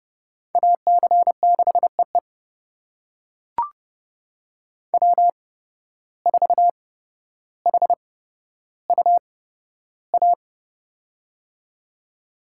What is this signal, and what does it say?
Morse code, heard as a keyed tone, about 30 wpm, spaced as for 13 wpm, 700 Hz. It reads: AC6EE W4HUA
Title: USA Call Signs - 5 Characters - 8x Character Spacing - 30wpm